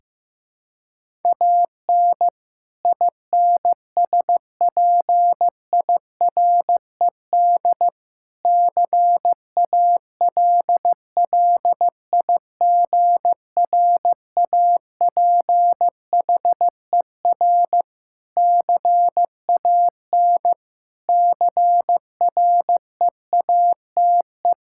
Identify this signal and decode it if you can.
Morse code, heard as a keyed tone, about 15 wpm, 700 Hz. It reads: AN INSPIRED CALLIGRAPHER CAN CREATE
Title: Pangram Challenge v2 - 15wpm